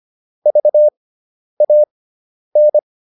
Morse code, heard as a keyed tone, about 25 wpm, spaced as for 10 wpm, 600 Hz. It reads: VAN